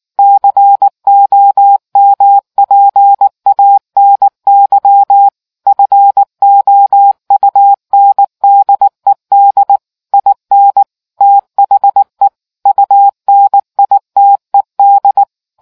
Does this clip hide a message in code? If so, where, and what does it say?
Yes: Morse code from 0.2 to 15.2 s, heard as a keyed tone, about 19 wpm, 800 Hz. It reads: COMPANY FOUNDED IN THE UNITED